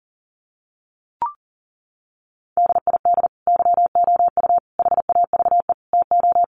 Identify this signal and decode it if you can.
Morse code, heard as a keyed tone, about 40 wpm, 700 Hz. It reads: DID YOU HAVE TO
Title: Sentences from Top 1500 Words 40wpm